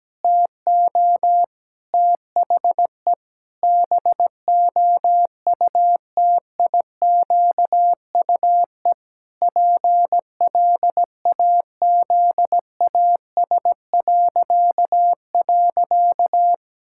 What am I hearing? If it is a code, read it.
Morse code, heard as a keyed tone, about 17 wpm, 700 Hz. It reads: TO THE BOUTIQUE PLAZAS..
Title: Pangram Challenge v2 - 17wpm